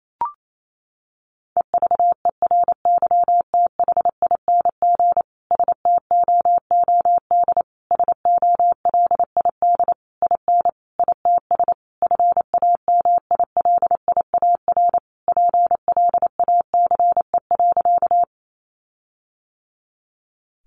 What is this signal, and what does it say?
Morse code, heard as a keyed tone, about 28 wpm, 700 Hz. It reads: EVERYTHING STOOD SOLID IN ITS FAMILIAR PLACE.
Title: Sentences from Top 1500 Words 28wpm